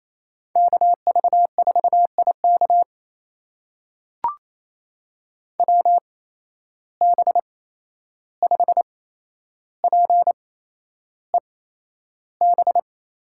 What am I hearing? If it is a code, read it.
Morse code, heard as a keyed tone, about 28 wpm, spaced as for 11 wpm, 700 Hz. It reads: KV4IK WB5PEB
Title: USA Call Signs - 8x Character Spacing - 28wpm